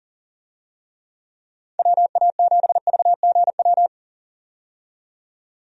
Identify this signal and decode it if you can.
Morse code, heard as a keyed tone, about 40 wpm, 700 Hz. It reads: WA7VGW
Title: USA Call Signs - 6 Characters - 40wpm